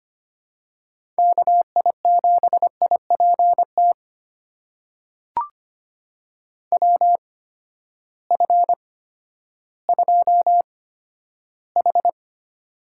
Morse code, heard as a keyed tone, about 25 wpm, spaced as for 12 wpm, 700 Hz. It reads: KI7IPT WF2H
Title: USA Call Signs - 8x Character Spacing - 25wpm